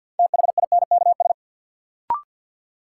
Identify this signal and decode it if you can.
Morse code, heard as a keyed tone, about 50 wpm, 700 Hz. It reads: THINKS
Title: Sentences from Top 1000 Words - Review Part 6 - New Word Drill - 50wpm